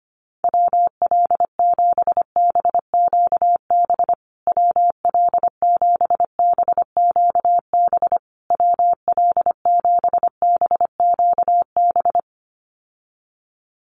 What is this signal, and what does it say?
Morse code, heard as a keyed tone, about 25 wpm, 700 Hz. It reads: WL7BQB WL7BQB WL7BQB
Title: USA Call Signs Repeated 3x 25wpm